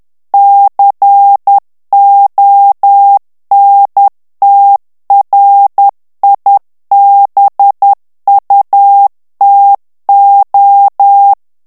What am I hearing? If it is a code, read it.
Morse code, heard as a keyed tone, about 10 wpm, 800 Hz. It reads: CONTRIBUTO
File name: morse mp3 / ITALIEN texte clair 1 vitesse 10 mots minute